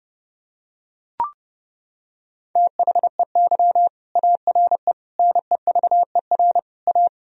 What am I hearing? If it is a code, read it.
Morse code, heard as a keyed tone, about 30 wpm, 700 Hz. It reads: THEY ARE NEVER A
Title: Sentences from Top 1000 Words - Review Part 4 - 30wpm